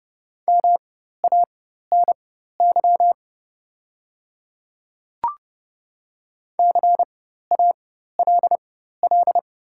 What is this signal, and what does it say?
Morse code, heard as a keyed tone, about 29 wpm, spaced as for 14 wpm, 700 Hz. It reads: MANY CALL
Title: 4x Character Spacing - Top 100 Words - 30wpm